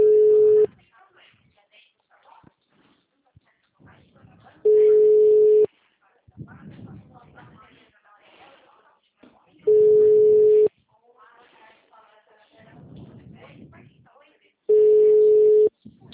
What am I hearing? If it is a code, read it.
Ringback tone (Europe, 425 Hz), with speech.